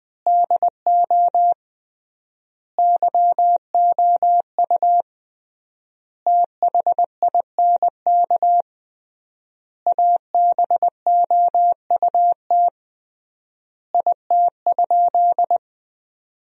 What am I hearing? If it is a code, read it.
Morse code, heard as a keyed tone, about 21 wpm, 700 Hz. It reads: DO YOU THINK ABOUT IT?
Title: 3x Word-Spacing - Sentences from Top 200 Words - 20wpm